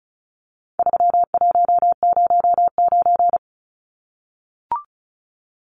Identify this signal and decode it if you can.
Morse code, heard as a keyed tone, about 34 wpm, 700 Hz. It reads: 3109
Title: Numbers - 4-Digit 35wpm